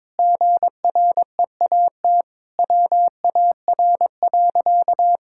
Morse code, heard as a keyed tone, about 22 wpm, 700 Hz. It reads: GREAT WAR.
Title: Sentences from Top 1600 Words 22wpm